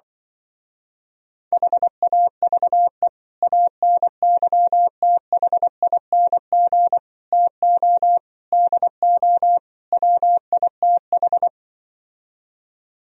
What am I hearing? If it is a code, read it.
Morse code, heard as a keyed tone, about 25 wpm, 700 Hz. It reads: HAVE ANYTHING TO DO WITH
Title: Speed-Racing - Sets of 5 Words 20wpm